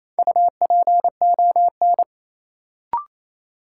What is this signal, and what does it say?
Morse code, heard as a keyed tone, about 29 wpm, 700 Hz. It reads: UPON
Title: Sentences from Top 1100 Words - New Word Drill Alt - 28wpm